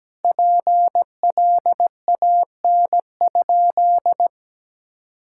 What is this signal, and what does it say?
Morse code, heard as a keyed tone, about 17 wpm, 700 Hz. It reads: PLAN?
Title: Sentences from Top 1000 Words - Review Part 6 - 17wpm